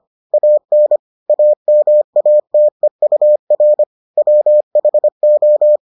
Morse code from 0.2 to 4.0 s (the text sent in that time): AN AMATEUR